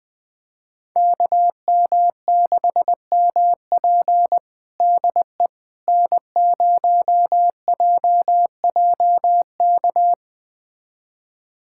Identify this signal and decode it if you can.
Morse code, heard as a keyed tone, about 21 wpm, 700 Hz. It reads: KM6MP DE N0JJK
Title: USA Callsign DE Callsign - 5 Characters - 20wpm